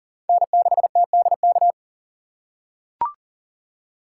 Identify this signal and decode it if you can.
Morse code, heard as a keyed tone, about 37 wpm, 700 Hz. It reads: N6TDK
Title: USA Call Signs - 5 Characters - 6x Character Spacing - 40wpm